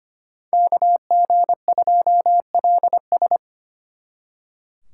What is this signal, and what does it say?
Morse code, heard as a keyed tone, about 33 wpm, 700 Hz. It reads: KG2LS